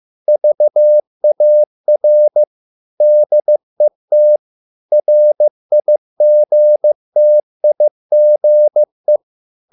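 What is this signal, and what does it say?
Morse code, heard as a keyed tone, about 15 wpm, 600 Hz. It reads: VAR DET RIGTIGE
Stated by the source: Morse code at a slow 15 wpm from CW med Gnister.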